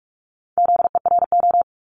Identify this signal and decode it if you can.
Morse code, heard as a keyed tone, about 46 wpm, 700 Hz. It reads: ZERO
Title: Multi-letter Phonemes 45wpm